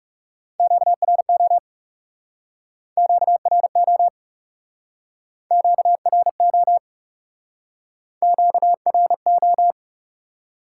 Morse code, heard as a keyed tone, about 40 wpm, 700 Hz. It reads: QRO QRO QRO QRO